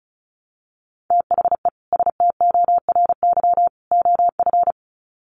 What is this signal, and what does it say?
Morse code, heard as a keyed tone, about 34 wpm, 700 Hz. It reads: THE STORY OF
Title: Sets of 3 Words 35wpm